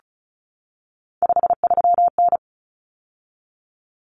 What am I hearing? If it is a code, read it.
Morse code, heard as a keyed tone, about 34 wpm, 700 Hz. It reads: 53N
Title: Common Cut Numbers - 3-Digit 35wpm